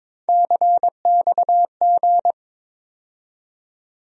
Morse code, heard as a keyed tone, about 22 wpm, 700 Hz. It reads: CXG
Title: USA Call Sign Postfixes - 3 Letters 22wpm